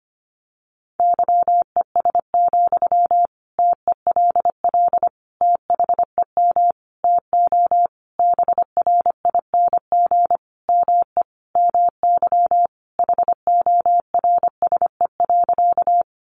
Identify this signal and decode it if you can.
Morse code, heard as a keyed tone, about 25 wpm, 700 Hz. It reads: YES, TELL THEM TO BRING ME MY HORSE.
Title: Sentences from Top 500 Words 25wpm